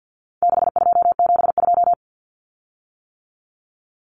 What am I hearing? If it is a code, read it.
Morse code, heard as a keyed tone, about 49 wpm, 700 Hz. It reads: 6272
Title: Common Cut Numbers - 4-Digit 50wpm